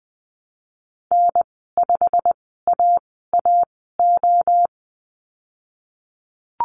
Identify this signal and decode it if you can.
Morse code, heard as a keyed tone, about 20 wpm, spaced as for 15 wpm, 700 Hz. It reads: N5AAO
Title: 088 Lesson 15g - Number 5 - Call Signs 20wpm